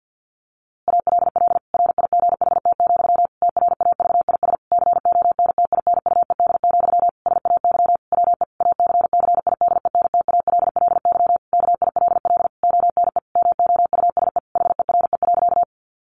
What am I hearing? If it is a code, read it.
Morse code, heard as a keyed tone, about 50 wpm, 700 Hz. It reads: ALL RIGHT, TRAVIS CONTINUED, SAY WE ACCIDENTALLY KILL ONE MOUSE HERE.
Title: A Sound Of Thunder 50wpm